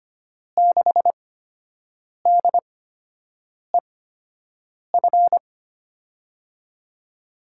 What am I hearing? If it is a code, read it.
Morse code, heard as a keyed tone, about 25 wpm, spaced as for 7 wpm, 700 Hz. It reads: 6DEF
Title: USA Call Signs - 6 Characters - 8x Character Spacing - 25wpm